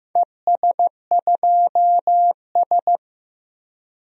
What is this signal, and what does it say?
Morse code, heard as a keyed tone, about 15 wpm, 700 Hz. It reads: ES2S
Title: DXCC-100 Call Signs - 4 Characters - 15wpm